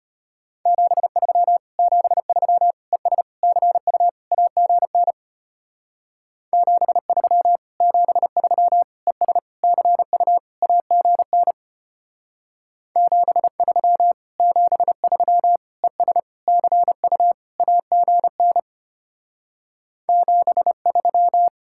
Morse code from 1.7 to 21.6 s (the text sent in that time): 73 ES CU AGN 73 73 ES CU AGN 73 73 ES CU AGN 73